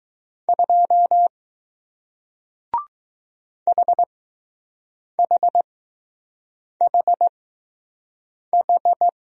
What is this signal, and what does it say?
Morse code, heard as a keyed tone, about 20 wpm, 700 Hz. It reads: 2 H H H H